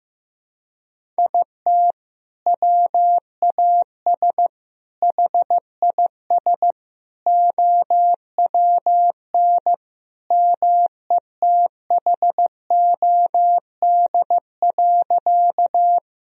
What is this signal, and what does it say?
Morse code, heard as a keyed tone, about 15 wpm, 700 Hz. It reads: IT WAS HIS OWN METHOD.
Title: Sentences from Top 1000 Words - Review Part 3 - 15wpm